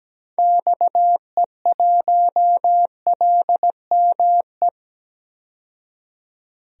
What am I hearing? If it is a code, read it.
Morse code, heard as a keyed tone, about 17 wpm, 700 Hz. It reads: XE1LME
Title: DXCC-100 Call Signs - 6 Characters - 17wpm